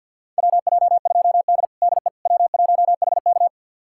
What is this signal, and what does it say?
Morse code, heard as a keyed tone, about 47 wpm, 700 Hz. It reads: WJ2D DE W1HK
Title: USA Callsign DE Callsign - 4 Characters - 50wpm